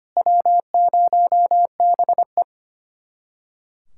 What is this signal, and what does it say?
Morse code, heard as a keyed tone, about 32 wpm, 700 Hz. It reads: W0BE